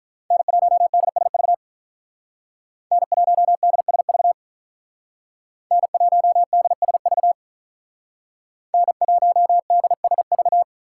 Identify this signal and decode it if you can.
Morse code, heard as a keyed tone, about 43 wpm, 700 Hz. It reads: N1DSV N1DSV N1DSV N1DSV
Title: Speed Racing - USA 1x3 Call Signs 35wpm